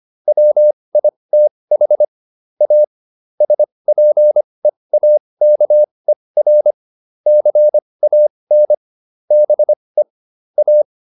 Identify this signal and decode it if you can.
Morse code, heard as a keyed tone, about 25 wpm, spaced as for 20 wpm, 600 Hz. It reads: WITH A SPEAKER CAN BE A